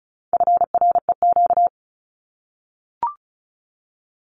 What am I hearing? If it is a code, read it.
Morse code, heard as a keyed tone, about 34 wpm, 700 Hz. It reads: FREQ
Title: Abbreviations Repeated 3x - 35wpm